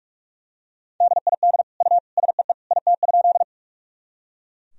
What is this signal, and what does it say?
Morse code, heard as a keyed tone, about 42 wpm, 700 Hz. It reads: DID U SEE IT?
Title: Sentences from Top 100 Words and Abbreviations 45wpm